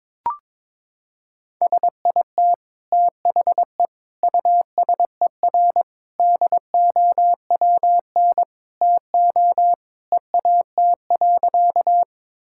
Morse code, heard as a keyed tone, about 22 wpm, 700 Hz. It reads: SIT THE USER DOWN TO EAT.
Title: Sentences from Top 1000 Words - Review Part 5 - 22wpm